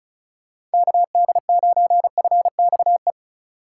Morse code, heard as a keyed tone, about 34 wpm, 690 Hz. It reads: KD9FXE
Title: USA Call Signs - 6 Characters - 4x Character Spacing - 35wpm